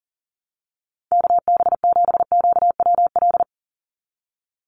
Morse code, heard as a keyed tone, about 40 wpm, 690 Hz. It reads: KB7QWL
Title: USA Call Signs 40wpm